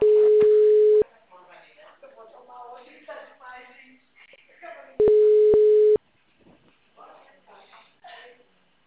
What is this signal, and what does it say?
Ringback tone (Europe, 425 Hz), with speech.